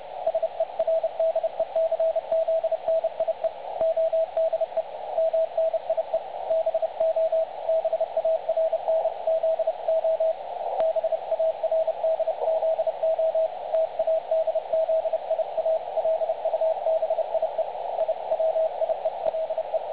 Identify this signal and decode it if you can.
Morse code, heard as a keyed tone, about 30 wpm, 660 Hz. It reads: SERDECZNIE ODE MNIE BO BARNZO BARDZO TADZIA LUBIE I WIELE
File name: dlazaawansowanych / nagraniazpasma / websdr recording 3565.0kHz